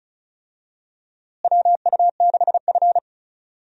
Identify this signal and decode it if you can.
Morse code, heard as a keyed tone, about 36 wpm, 700 Hz. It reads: WU6F
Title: USA Call Signs - 4 Characters - Repeated 2x - 35wpm